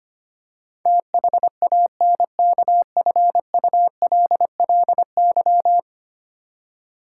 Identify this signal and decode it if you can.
Morse code, heard as a keyed tone, about 25 wpm, 700 Hz. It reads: THANKFULLY
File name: Ten Letter Words - 25wpm